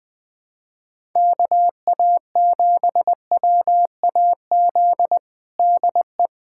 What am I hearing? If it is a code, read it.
Morse code, heard as a keyed tone, about 20 wpm, 700 Hz. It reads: KA7WAZ DE
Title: USA Callsign DE Callsign - 6 Characters - 20wpm